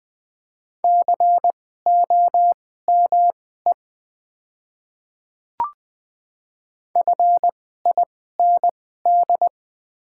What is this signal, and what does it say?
Morse code, heard as a keyed tone, about 20 wpm, spaced as for 14 wpm, 700 Hz. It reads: COME FIND
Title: 2x Character Spacing - Top 300 Words - 20wpm